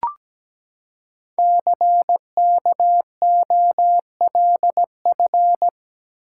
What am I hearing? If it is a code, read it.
Morse code, heard as a keyed tone, about 17 wpm, 700 Hz. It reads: CKOLF
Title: Code Groups - 5 Letters - 17wpm